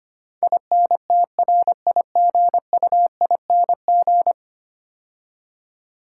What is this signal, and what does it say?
Morse code, heard as a keyed tone, about 25 wpm, 700 Hz. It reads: INTRIGUING
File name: 10 Letter Words - First 3 Letters Given - 25wpm